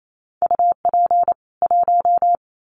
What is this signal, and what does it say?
Morse code, heard as a keyed tone, about 28 wpm, 700 Hz. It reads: UP 1
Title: Common QSO Phrases 28wpm